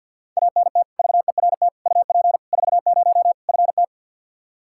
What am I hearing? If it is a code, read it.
Morse code, heard as a keyed tone, about 46 wpm, 700 Hz. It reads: ANT VERT UP 40 FT
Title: Practice QSO - Exchange 2 - Info - 50wpm